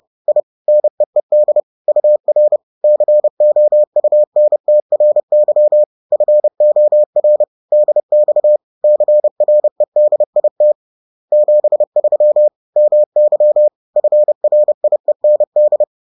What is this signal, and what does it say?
Morse code, heard as a keyed tone, about 30 wpm, 600 Hz. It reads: I NEED UR COUNTRY FOR DX CREDIT 73 MY FRIEND